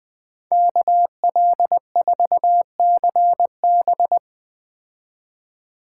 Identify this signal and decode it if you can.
Morse code, heard as a keyed tone, about 20 wpm, 700 Hz. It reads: KL4CB
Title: Speed Racing - USA Call Signs 17wpm